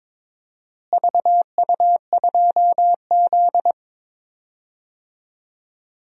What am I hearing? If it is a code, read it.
Morse code, heard as a keyed tone, about 22 wpm, 700 Hz. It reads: VU2Z
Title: DXCC-100 Call Signs - 4 Characters - Repeated 3x - 22wpm